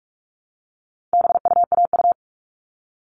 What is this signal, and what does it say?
Morse code, heard as a keyed tone, about 44 wpm, 700 Hz. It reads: BUAU
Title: All Cut Numbers - 4-Digit 45wpm